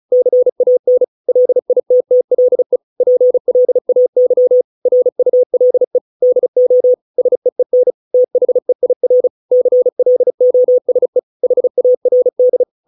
Morse code, heard as a keyed tone, about 34 wpm, 500 Hz. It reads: CAN LITTLE PLAY RULE DO SEEN THEIR CLOSE HARD